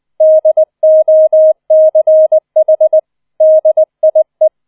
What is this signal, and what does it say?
Morse code, heard as a keyed tone, about 20 wpm, 630 Hz. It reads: DOCH DIE